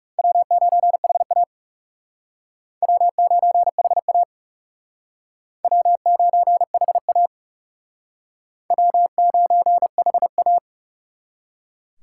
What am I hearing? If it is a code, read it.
Morse code, heard as a keyed tone, about 37 wpm, 700 Hz. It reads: W9HA W9HA W9HA W9HA
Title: Speed Racing - USA 1x2 Call Signs 30wpm